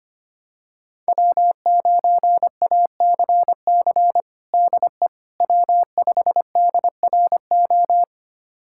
Morse code, heard as a keyed tone, about 25 wpm, 700 Hz. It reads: W9ACC DE W5DRO
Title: USA Callsign DE Callsign - 5 Characters - 25wpm